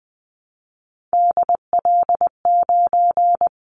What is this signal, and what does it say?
Morse code, heard as a keyed tone, about 20 wpm, 700 Hz. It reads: DL9
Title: Callsign Prefixes for Top 25 DXCC - Repeated 3x - 20wpm